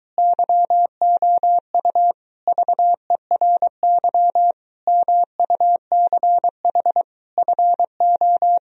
Morse code, heard as a keyed tone, about 24 wpm, 700 Hz. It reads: YOU VERY MUCH FO